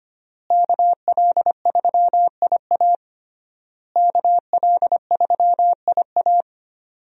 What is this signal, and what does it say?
Morse code, heard as a keyed tone, about 25 wpm, 700 Hz. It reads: KL3IA KL3IA
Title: USA Call Signs - 5 Characters - Repeated 3x - 25wpm